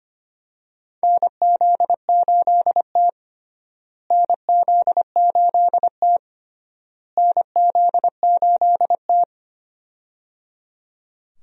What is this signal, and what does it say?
Morse code, heard as a keyed tone, about 25 wpm, 700 Hz. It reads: NZ8T NZ8T NZ8T
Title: USA Call Signs - 4 Characters - Repeated 3x - 25wpm